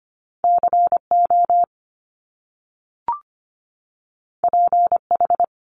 Morse code, heard as a keyed tone, about 25 wpm, 700 Hz. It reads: CO PH
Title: Top Two Letter Combinations in Words 25wpm